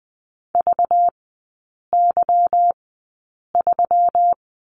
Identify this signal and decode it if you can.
Morse code, heard as a keyed tone, about 20 wpm, spaced as for 8 wpm, 700 Hz. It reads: VY3
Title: 198 Lesson 32e - Letter k - Three Characters 20wpm